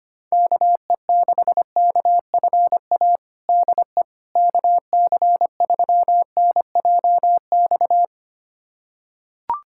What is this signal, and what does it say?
Morse code, heard as a keyed tone, about 25 wpm, 700 Hz. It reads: KE6KFA DE KC3NJX
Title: USA Callsign DE Callsign - 6 Characters - 25wpm